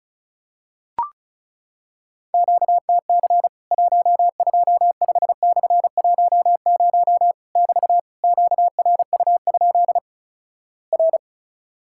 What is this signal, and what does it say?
Morse code, heard as a keyed tone, about 36 wpm, 700 Hz. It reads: QTC 125/10 <BT> QRU? R
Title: Worked All Europe DX Contest - QTC Exchange - 35wpm